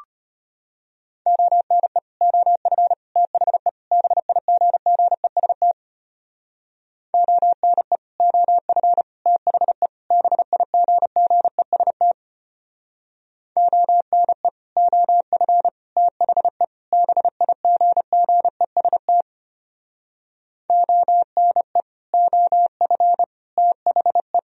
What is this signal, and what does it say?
Morse code, heard as a keyed tone, about 30 wpm, 700 Hz. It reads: ONE OF THE BIGGEST ONE OF THE BIGGEST ONE OF THE BIGGEST ONE OF THE
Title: Speed-Racing - Sets of 4 Words 25wpm